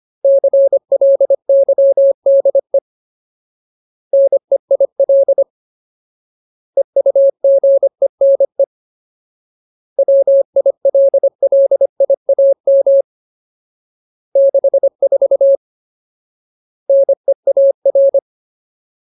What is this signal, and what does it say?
Morse code, heard as a keyed tone, about 25 wpm, 550 Hz. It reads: CLYDE NEIL EUGENE WILLIAM 64 NEAR
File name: Ultimate Word List / 25WPM Ultimate